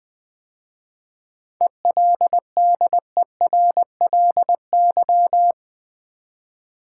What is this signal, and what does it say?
Morse code, heard as a keyed tone, about 20 wpm, 700 Hz. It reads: ELDERLY
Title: 7 Letter Words - First 3 Letters Given - 20wpm